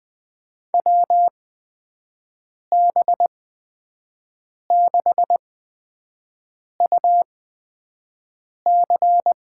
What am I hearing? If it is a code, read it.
Morse code, heard as a keyed tone, about 20 wpm, spaced as for 5 wpm, 700 Hz. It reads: WB6UC
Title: USA Call Signs - 8x Character Spacing - 20wpm